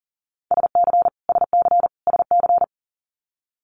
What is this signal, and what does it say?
Morse code, heard as a keyed tone, about 40 wpm, 700 Hz. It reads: SC SC SC
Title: Multi-letter Phonemes Repeated 3x 40wpm